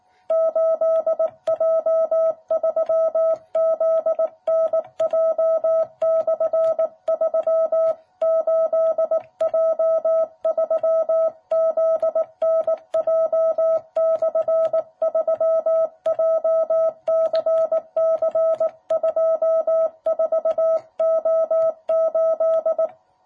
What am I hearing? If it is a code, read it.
Morse code, heard as a keyed tone, about 18 wpm, 660 Hz. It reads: 8J3ZNJ/3 8J3ZNJ/3 JCC24O8
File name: jh8cbh / 2021cw